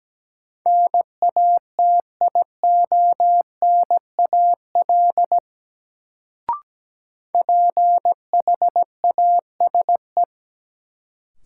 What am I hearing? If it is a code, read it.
Morse code, heard as a keyed tone, about 17 wpm, 700 Hz. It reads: NATIONAL PHASE
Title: Sentences from Top 1200 Words - New Word Drill Alt - 17wpm